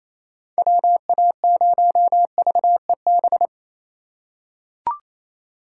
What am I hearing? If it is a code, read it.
Morse code, heard as a keyed tone, about 28 wpm, 700 Hz. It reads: WA0VEB
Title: USA Call Signs - 6 Characters - 3x Character Spacing - 28wpm